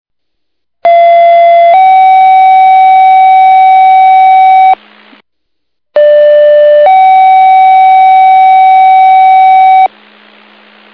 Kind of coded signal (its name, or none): two-tone page